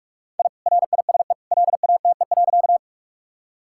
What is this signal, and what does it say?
Morse code, heard as a keyed tone, about 42 wpm, 700 Hz. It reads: I RISE LATE.
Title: Sentences from Top 1000 Words - Review Part 6 - 45wpm